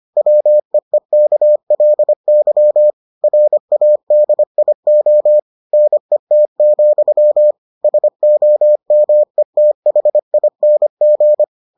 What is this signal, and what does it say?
Morse code, heard as a keyed tone, about 25 wpm, 600 Hz. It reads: WEEKLY RADIO NET, SOMETHING